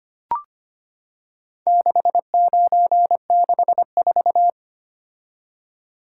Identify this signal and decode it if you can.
Morse code, heard as a keyed tone, about 25 wpm, 700 Hz. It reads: 6964
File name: practice / 25wpm / Four Letter-Number Combinations - 25wpm